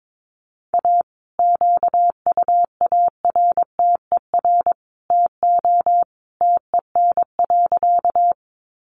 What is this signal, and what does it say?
Morse code, heard as a keyed tone, about 22 wpm, 700 Hz. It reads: A QUARTER TO TEN.
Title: Sentences from Top 1300 Words 22wpm